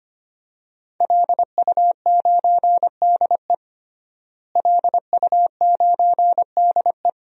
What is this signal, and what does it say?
Morse code, heard as a keyed tone, about 25 wpm, 700 Hz. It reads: LU9DE LU9DE
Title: DXCC-100 Call Signs - 5 Characters - Repeated 3x - 25wpm